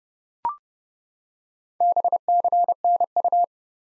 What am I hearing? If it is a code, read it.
Morse code, heard as a keyed tone, about 30 wpm, 700 Hz. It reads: BCNU
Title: Top 90 Abbr 30WPM